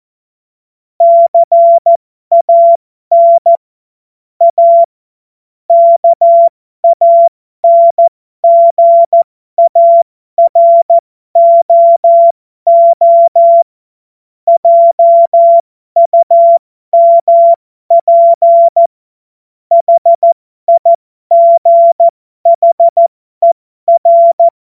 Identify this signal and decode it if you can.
Morse code, heard as a keyed tone, about 14 wpm, spaced as for 12 wpm, 680 Hz. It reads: CAN A KANGAROO JUMP HIGHER
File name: Morsekurs / Witze